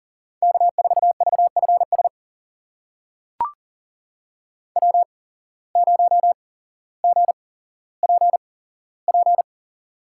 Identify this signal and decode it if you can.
Morse code, heard as a keyed tone, about 40 wpm, spaced as for 17 wpm, 700 Hz. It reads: K4VFS W0GPP
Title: USA Call Signs - 5 Characters - 8x Character Spacing - 40wpm